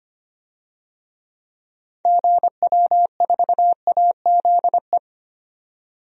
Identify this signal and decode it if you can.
Morse code, heard as a keyed tone, about 25 wpm, 700 Hz. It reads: GW4AZE
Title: DXCC-100 Call Signs - 6 Characters - 25wpm